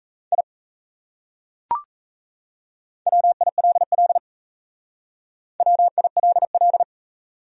Morse code, heard as a keyed tone, about 35 wpm, 700 Hz. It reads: I WILL WILL
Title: Speed-Racing Top 300 Words 28wpm